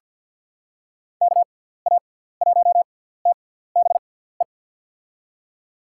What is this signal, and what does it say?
Morse code, heard as a keyed tone, about 53 wpm, spaced as for 17 wpm, 700 Hz. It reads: KA1TBE
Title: USA Call Signs - 6 Characters - 6x Character Spacing - 50wpm